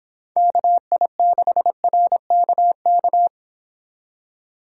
Morse code, heard as a keyed tone, about 25 wpm, 700 Hz. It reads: KI6RKK